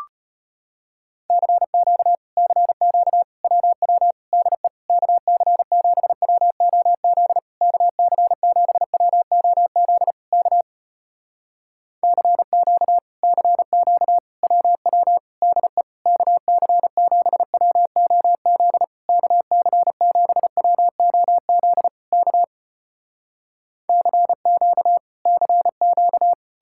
Morse code, heard as a keyed tone, about 35 wpm, 700 Hz. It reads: CQ CQ WW DE KC7WOZ KC7WOZ K CQ CQ WW DE KC7WOZ KC7WOZ K CQ CQ